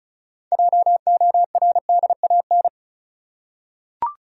Morse code, heard as a keyed tone, about 36 wpm, 700 Hz. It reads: JORDAN
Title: USA Common Names - 35wpm